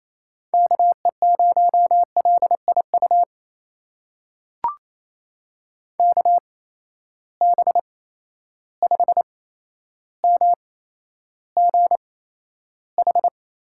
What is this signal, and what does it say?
Morse code, heard as a keyed tone, about 28 wpm, spaced as for 12 wpm, 700 Hz. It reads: KE0LIU KB5MGH
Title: USA Call Signs - 6 Characters - 8x Character Spacing - 28wpm